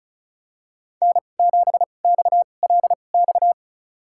Morse code, heard as a keyed tone, about 34 wpm, spaced as for 25 wpm, 700 Hz. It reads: N7XLX
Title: USA Call Signs - 2x Character Spacing - 35wpm